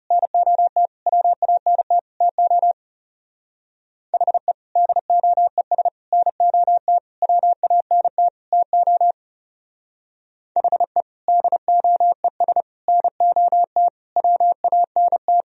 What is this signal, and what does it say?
Morse code, heard as a keyed tone, about 36 wpm, 700 Hz. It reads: NOT WANT TO HE DOES NOT WANT TO HE DOES NOT WANT
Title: Speed-Racing - Sets of 5 Words 30wpm